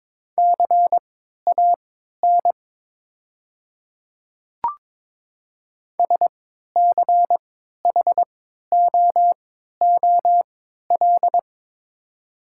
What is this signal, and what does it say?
Morse code, heard as a keyed tone, about 22 wpm, spaced as for 12 wpm, 700 Hz. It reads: CAN SCHOOL
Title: 3x Character Spacing - Top 300 Words - 22wpm